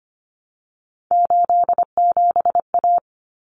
Morse code, heard as a keyed tone, about 25 wpm, 700 Hz. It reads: 87A
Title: Common Cut Numbers - 3-Digit 25wpm